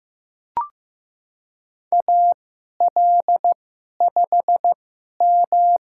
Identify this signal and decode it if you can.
Morse code, heard as a keyed tone, about 15 wpm, spaced as for 11 wpm, 700 Hz. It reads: AL5M
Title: USA Call Signs - 4 Characters - 2x Character Spacing - 15wpm